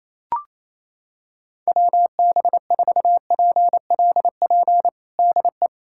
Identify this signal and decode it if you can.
Morse code, heard as a keyed tone, about 28 wpm, 700 Hz. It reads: WB4PLP DE
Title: USA Callsign DE Callsign 28wpm